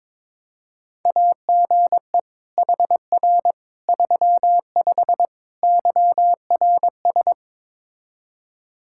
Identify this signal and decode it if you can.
Morse code, heard as a keyed tone, about 21 wpm, 700 Hz. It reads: AGE HR 35 YRS
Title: K1USN SST QSO Practice 22wpm